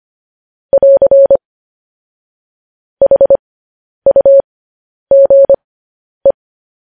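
Morse code, heard as a keyed tone, about 25 wpm, spaced as for 10 wpm, 550 Hz. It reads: <AR> HUGE
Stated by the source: Morse code practice of news headlines at 10 wpm.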